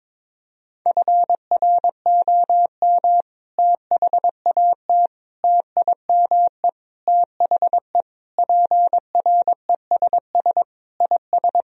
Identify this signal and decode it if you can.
Morse code, heard as a keyed tone, about 22 wpm, 700 Hz. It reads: FROM THAT TIME THE PRESS IS
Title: Sentences from Top 1000 Words - Review Part 1 - 22wpm